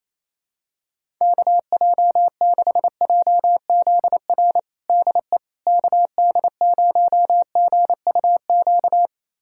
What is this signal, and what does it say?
Morse code, heard as a keyed tone, about 28 wpm, 700 Hz. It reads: KJ6JZR DE KD0GUQ
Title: USA Callsign DE Callsign 28wpm